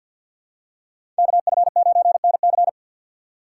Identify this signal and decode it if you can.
Morse code, heard as a keyed tone, about 51 wpm, 700 Hz. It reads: KF9NC
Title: USA Call Signs - 5 Characters - 3x Character Spacing - 50wpm